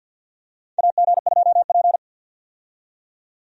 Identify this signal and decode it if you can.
Morse code, heard as a keyed tone, about 50 wpm, 700 Hz. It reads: AG2P
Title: USA Call Signs - 4 Characters - 50wpm